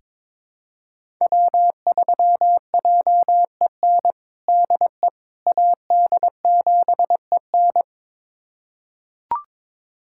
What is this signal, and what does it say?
Morse code, heard as a keyed tone, about 21 wpm, 700 Hz. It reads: W3JEN DE AD7EN
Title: USA Callsign DE Callsign - 5 Characters - 22wpm